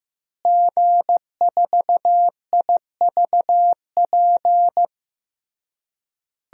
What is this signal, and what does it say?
Morse code, heard as a keyed tone, about 15 wpm, 700 Hz. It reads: G4IVP
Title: DXCC-100 Call Signs - 5 Characters - Repeated 3x - 15wpm